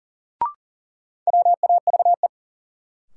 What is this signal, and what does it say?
Morse code, heard as a keyed tone, about 40 wpm, 700 Hz. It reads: WAVE
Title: Sentences from Top 1400 Words - New Word Drill - 40wpm